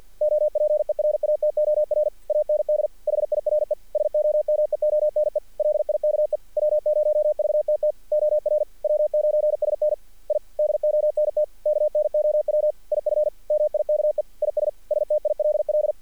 Morse code, heard as a keyed tone, about 48 wpm, 600 Hz. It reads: OPERATOR AND HIRE SOMEONE LIKE W0VTT OR W9SN I DONT KNOW IF MIKE IS STILL
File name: archived / CRSnet